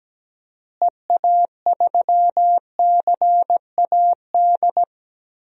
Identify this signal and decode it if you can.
Morse code, heard as a keyed tone, about 17 wpm, 700 Hz. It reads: EA3CAD
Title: DXCC-100 Call Signs - 6 Characters - Repeated 3x - 17wpm